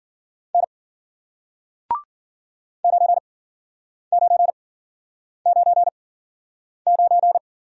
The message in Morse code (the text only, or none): N 9999